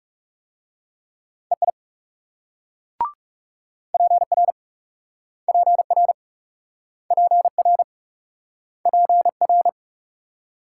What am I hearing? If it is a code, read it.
Morse code, heard as a keyed tone, about 36 wpm, 700 Hz. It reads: EI PR PR PR PR
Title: Speed-Racing Multi-letter Phonemes - 30wpm